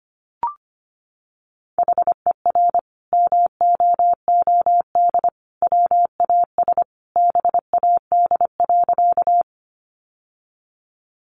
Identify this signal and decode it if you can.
Morse code, heard as a keyed tone, about 25 wpm, 700 Hz. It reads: HER MOOD WAS BAD.
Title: Sentences from Top 1100 Words 25wpm